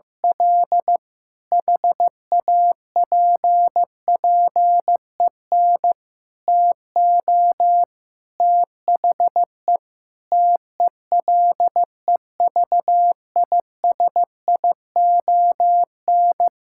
Morse code, heard as a keyed tone, about 15 wpm, 700 Hz. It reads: L HAPPEN TO THE TELEVISION
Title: Sentences from Top 1000 Words - Review Part 5 - 15wpm